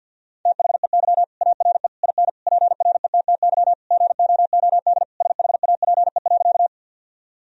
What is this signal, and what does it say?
Morse code, heard as a keyed tone, about 50 wpm, 700 Hz. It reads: THEY ARE IN PRETTY GOOD SHAPE.
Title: Sentences from Top 1000 Words - Review Part 3 - 50wpm